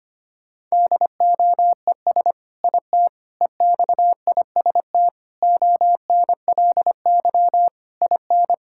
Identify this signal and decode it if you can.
Morse code, heard as a keyed tone, about 25 wpm, 700 Hz. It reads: DOES IT EXIST ONLY IN
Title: Sentences from Top 1000 Words - Review Part 5 - 25wpm